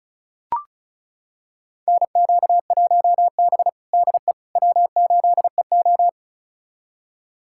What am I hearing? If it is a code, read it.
Morse code, heard as a keyed tone, about 36 wpm, 700 Hz. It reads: NQ1B DE W8EO
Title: USA Callsign DE Callsign - 4 Characters - 35wpm